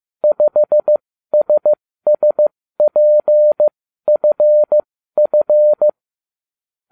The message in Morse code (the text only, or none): 5SSPFF